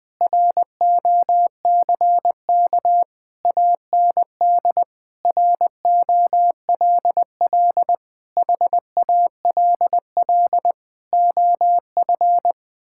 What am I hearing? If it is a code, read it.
Morse code, heard as a keyed tone, about 20 wpm, 700 Hz. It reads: ROCK AND ROLL HALL OF
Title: Sentences with 6-Word Phrases - 20wpm